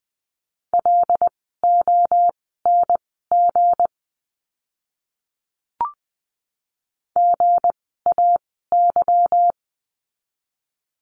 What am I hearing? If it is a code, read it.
Morse code, heard as a keyed tone, about 20 wpm, spaced as for 15 wpm, 700 Hz. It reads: LONG GAY
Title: 160 Lesson 26f - Letter g - Words 20wpm